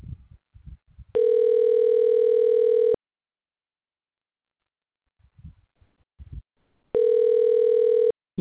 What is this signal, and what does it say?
Ringback tone (North America), 440 + 480 Hz.
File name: recordings / RE37d61cc2c35ac1a5b8663dee745fae5b